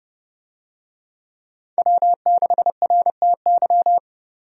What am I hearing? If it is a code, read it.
Morse code, heard as a keyed tone, about 29 wpm, 700 Hz. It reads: W6RTY
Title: USA Call Signs - 5 Characters - 30wpm